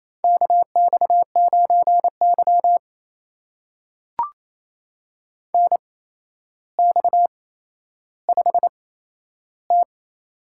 Morse code, heard as a keyed tone, about 28 wpm, spaced as for 12 wpm, 700 Hz. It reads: KX9Y NX5T
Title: USA Call Signs - 4 Characters - 8x Character Spacing - 28wpm